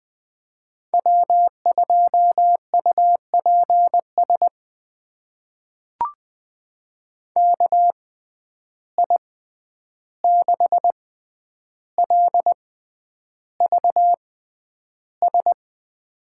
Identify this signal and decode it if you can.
Morse code, heard as a keyed tone, about 20 wpm, spaced as for 10 wpm, 700 Hz. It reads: W2UPS KI6LVS
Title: USA Call Signs - 6x Character Spacing - 20wpm